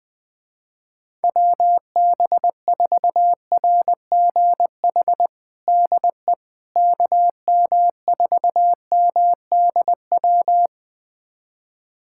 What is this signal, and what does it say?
Morse code, heard as a keyed tone, about 20 wpm, 700 Hz. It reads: WB4RGH DE KM4MDW
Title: USA Callsign DE Callsign 20wpm